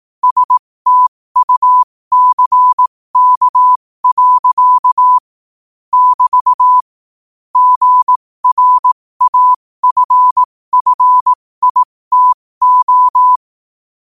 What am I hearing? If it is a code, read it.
Morse code, heard as a keyed tone, about 18 wpm, spaced as for 15 wpm, 1000 Hz. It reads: STUCK. <BT> GRAFFITO
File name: qotdfiles / 15wpm